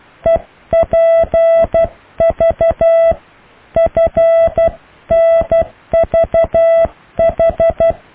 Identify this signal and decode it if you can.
Morse code, heard as a keyed tone, about 12 wpm, 660 Hz. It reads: EPV FNVH
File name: cw / EKTPV+BFHNW